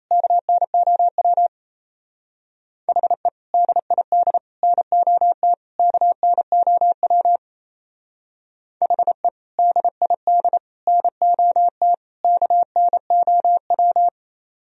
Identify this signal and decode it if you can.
Morse code, heard as a keyed tone, about 35 wpm, 700 Hz. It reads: KNOW HE DID NOT KNOW HE DID NOT KNOW
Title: Speed-Racing - Sets of 4 Words 28wpm